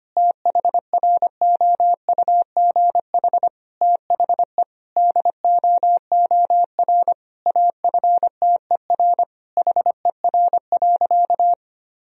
Morse code, heard as a keyed tone, about 25 wpm, 700 Hz. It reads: THROUGH THE DOOR AFTER HER.
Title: Sentences from Top 1000 Words - Review Part 1 - 25wpm